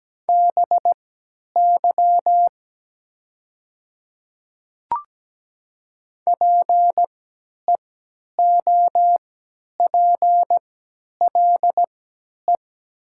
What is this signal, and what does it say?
Morse code, heard as a keyed tone, about 17 wpm, spaced as for 10 wpm, 700 Hz. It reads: BY PEOPLE
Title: 3x Character Spacing - Top 200 Words - 17wpm